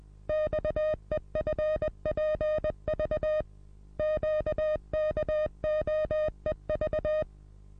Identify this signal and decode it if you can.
Morse code, heard as a keyed tone, about 20 wpm, 620 Hz. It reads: XEFPV QKOEV
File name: repos / Morsekurs / DJ4UF Morsen27